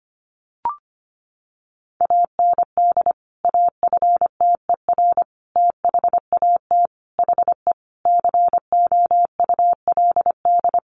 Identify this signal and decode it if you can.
Morse code, heard as a keyed tone, about 25 wpm, 700 Hz. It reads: AND AFTER THAT HE COULD
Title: Sentences from Top 1400 Words 25wpm